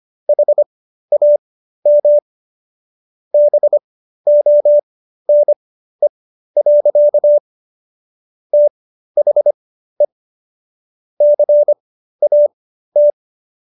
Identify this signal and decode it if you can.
Morse code, heard as a keyed tone, about 25 wpm, spaced as for 13 wpm, 590 Hz. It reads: HAM BONE. THE CAT